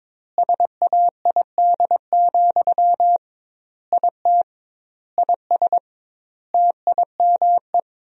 Morse code, heard as a keyed tone, about 22 wpm, 700 Hz. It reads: SAID, IT IS TIME
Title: 2x Word-Spacing - Sentences from Top 300 Words - 22wpm